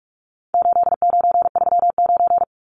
Morse code, heard as a keyed tone, about 44 wpm, 700 Hz. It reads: 8939
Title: Numbers - 4-Digit 45wpm